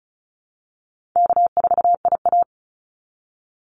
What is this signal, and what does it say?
Morse code, heard as a keyed tone, about 34 wpm, 700 Hz. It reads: K4IA
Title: Easy Call Signs Repeated 3x - 35wpm